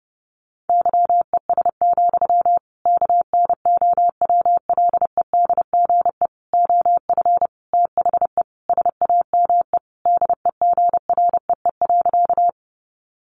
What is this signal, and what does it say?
Morse code, heard as a keyed tone, about 30 wpm, 700 Hz. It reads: YES, KNOWLEDGE OF THE SAME DEGREE.
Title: Sentences from Top 1500 Words 30wpm